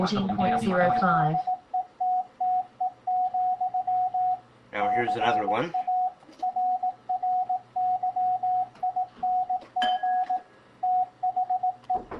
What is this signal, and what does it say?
Morse code, heard as a keyed tone, about 18 wpm, 740 Hz, with speech and radio noise. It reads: TZETTE, CARRYING THE